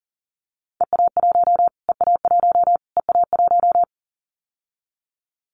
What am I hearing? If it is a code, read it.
Morse code, heard as a keyed tone, about 40 wpm, 700 Hz. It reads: EA1 EA1 EA1
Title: Callsign Prefixes for Top 25 DXCC - Repeated 3x - 40wpm